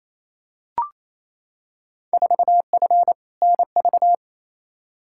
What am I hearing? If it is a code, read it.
Morse code, heard as a keyed tone, about 28 wpm, 700 Hz. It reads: 4F NV
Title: Field Day Exchanges Encore - 28wpm